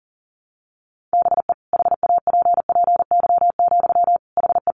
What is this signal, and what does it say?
Morse code, heard as a keyed tone, about 40 wpm, 700 Hz. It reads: BE HAPPY, HE
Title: Sentences from Top 1200 Words 40wpm